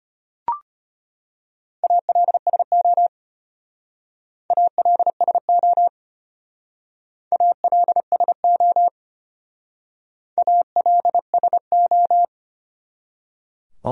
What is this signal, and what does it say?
Morse code, heard as a keyed tone, about 33 wpm, 700 Hz. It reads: ALSO ALSO ALSO ALSO